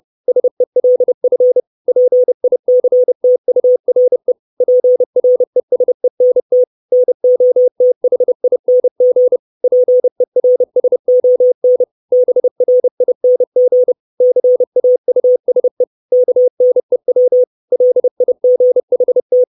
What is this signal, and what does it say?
Morse code, heard as a keyed tone, about 30 wpm, 500 Hz. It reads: SELF PICTURE PRESENT NOTHING PERSON BRING CAUSE KNEW LIGHT
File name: 632 Words 30WPM